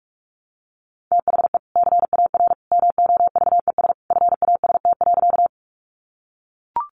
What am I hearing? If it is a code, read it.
Morse code, heard as a keyed tone, about 44 wpm, 700 Hz. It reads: THE CAR MOVES FAST.
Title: Easy Phrases 45wpm